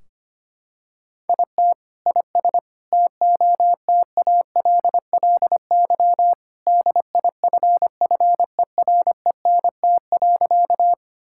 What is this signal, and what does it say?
Morse code, heard as a keyed tone, about 25 wpm, 700 Hz. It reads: IT IS TOTALLY DIFFERENT.
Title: Sentences from Top 1600 Words 25wpm